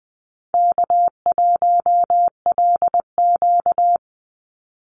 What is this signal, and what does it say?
Morse code, heard as a keyed tone, about 20 wpm, 700 Hz. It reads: K1LQ
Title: USA Call Signs 20wpm